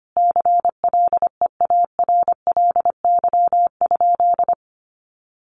Morse code, heard as a keyed tone, about 25 wpm, 700 Hz. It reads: CLEARLY?
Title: Sentences from Top 1300 Words 25wpm